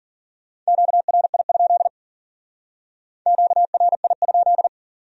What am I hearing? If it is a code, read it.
Morse code, heard as a keyed tone, about 52 wpm, 700 Hz. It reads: QRI? QRI?